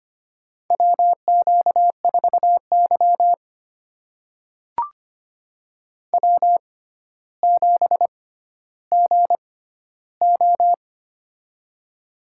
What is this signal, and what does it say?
Morse code, heard as a keyed tone, about 25 wpm, spaced as for 13 wpm, 700 Hz. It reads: WQ4Y W7GO
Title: USA Call Signs - 4 Characters - 6x Character Spacing - 25wpm